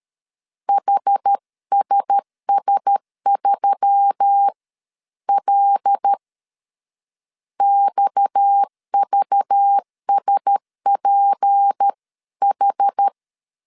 Morse code, heard as a keyed tone, about 13 wpm, spaced as for 11 wpm, 800 Hz. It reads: HSS3L XVSPH
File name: CW / DET / 1 6 1500C S B 12wpm 3431